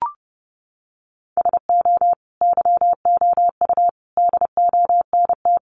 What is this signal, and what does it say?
Morse code, heard as a keyed tone, about 30 wpm, 700 Hz. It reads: SO YOU DONT
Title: Sentences from Top 1000 Words 30wpm